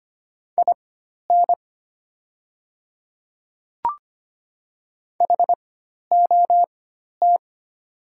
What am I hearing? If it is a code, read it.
Morse code, heard as a keyed tone, about 25 wpm, spaced as for 12 wpm, 700 Hz. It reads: IN HOT
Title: 4x Character Spacing - Top 100 Words - 25wpm